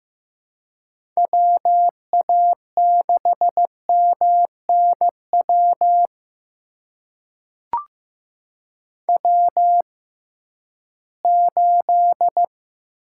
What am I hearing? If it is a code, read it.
Morse code, heard as a keyed tone, about 15 wpm, spaced as for 11 wpm, 700 Hz. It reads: WA6MNW W8
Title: USA Call Signs - 6x Character Spacing - 15wpm